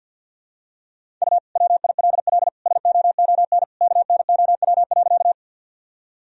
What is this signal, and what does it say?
Morse code, heard as a keyed tone, about 51 wpm, 700 Hz. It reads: U WILL SOON KNOW.
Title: Sentences from Top 400 Words and Abbreviations 50wpm